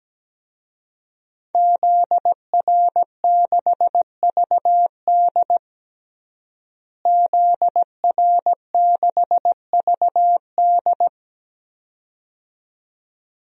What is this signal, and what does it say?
Morse code, heard as a keyed tone, about 17 wpm, 700 Hz. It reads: ZR6VD ZR6VD
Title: DXCC-100 Call Signs - 5 Characters - Repeated 2x - 17wpm